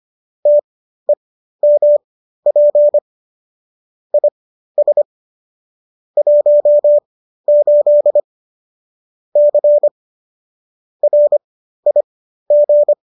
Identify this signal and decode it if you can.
Morse code, heard as a keyed tone, about 25 wpm, spaced as for 13 wpm, 590 Hz. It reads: TEMP IS 18 C RIG